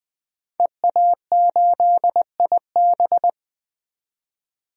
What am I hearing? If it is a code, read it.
Morse code, heard as a keyed tone, about 20 wpm, 700 Hz. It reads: EA8IB
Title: DXCC-100 Call Signs - 5 Characters - 20wpm